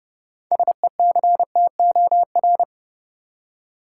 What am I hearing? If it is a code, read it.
Morse code, heard as a keyed tone, about 30 wpm, 700 Hz. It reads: SECTOR
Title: Six Letter Words - 30wpm